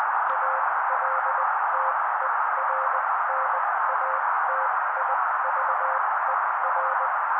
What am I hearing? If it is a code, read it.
Morse code, heard as a keyed tone, about 19 wpm, spaced as for 16 wpm, 530 Hz. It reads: ALTERNATIVER
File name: daily / 16wpm noise